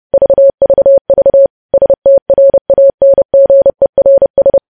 Morse code, heard as a keyed tone, about 30 wpm, 550 Hz. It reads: VVV STRANGERS
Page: news headlines in morse code at 30 wpm